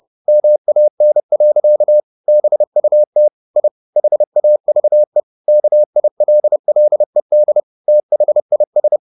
Morse code, heard as a keyed tone, about 30 wpm, 600 Hz. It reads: MAN. BUT I HAVE KILLED THIS